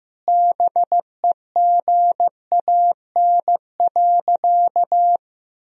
Morse code, heard as a keyed tone, about 14 wpm, 700 Hz. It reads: BEGAN.
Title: Sentences from Top 400 Words and Abbreviations 15wpm